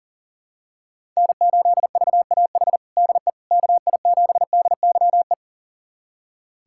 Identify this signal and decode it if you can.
Morse code, heard as a keyed tone, about 40 wpm, 700 Hz. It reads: N8VAH DE KI7DYE
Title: USA Callsign DE Callsign 40wpm